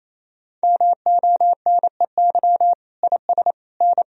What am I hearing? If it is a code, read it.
Morse code, heard as a keyed tone, about 28 wpm, 700 Hz. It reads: MONEY IS N
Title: Sentences from Top 500 Words and Abbreviations 28wpm